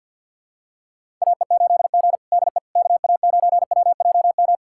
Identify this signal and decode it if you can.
Morse code, heard as a keyed tone, about 49 wpm, 700 Hz. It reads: AE8G DE KA9WJM
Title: USA Callsign DE Callsign 50wpm